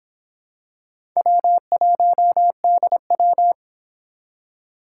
Morse code, heard as a keyed tone, about 23 wpm, 700 Hz. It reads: W1DW